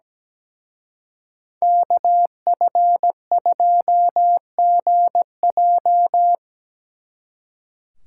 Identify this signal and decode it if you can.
Morse code, heard as a keyed tone, about 20 wpm, 700 Hz. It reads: KF2GJ
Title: Speed Racing - USA Call Signs 17wpm